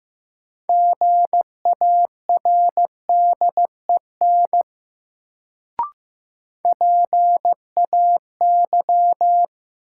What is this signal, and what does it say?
Morse code, heard as a keyed tone, about 15 wpm, 700 Hz. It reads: GARDEN PAY
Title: Sentences from Top 700 Words - New Word Drill - 15wpm